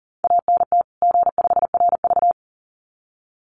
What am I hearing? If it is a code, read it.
Morse code, heard as a keyed tone, about 40 wpm, 700 Hz. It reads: ANT G5RV
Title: Common QSO Phrases 40wpm